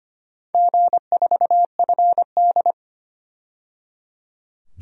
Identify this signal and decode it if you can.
Morse code, heard as a keyed tone, about 25 wpm, 700 Hz. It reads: G4FD
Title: DXCC-100 Call Signs - 4 Characters - 25wpm